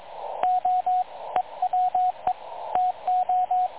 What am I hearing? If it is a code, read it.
Morse code, heard as a keyed tone, about 22 wpm, 730 Hz, with radio noise. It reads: O EWE TO
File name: dlazaawansowanych / nagraniazpasma / websdr recording 3567.0kHz